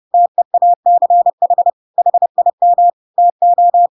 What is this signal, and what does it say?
Morse code, heard as a keyed tone, about 30 wpm, 700 Hz. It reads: TEACH HIM TO
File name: Pinocchio / Pinocchio ch3 30 30 ik1hge